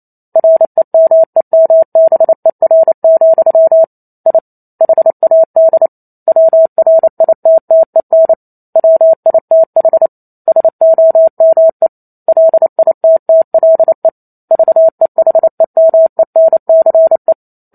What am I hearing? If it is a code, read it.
Morse code, heard as a keyed tone, about 28 wpm, 650 Hz. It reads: REMEMBER, I HAD WRITTEN WITH SOME LITTLE VEHEMENCE
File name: Chapter 2 - 28 WPM